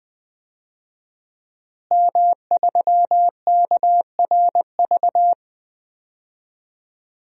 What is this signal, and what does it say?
Morse code, heard as a keyed tone, about 20 wpm, 700 Hz. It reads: M3KRV
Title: DXCC-100 Call Signs - 5 Characters - Repeated 2x - 20wpm